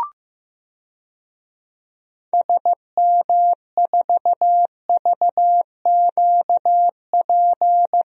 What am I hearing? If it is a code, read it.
Morse code, heard as a keyed tone, about 15 wpm, 700 Hz. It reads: SM4VQP
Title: DXCC-100 Call Signs - 6 Characters - 15wpm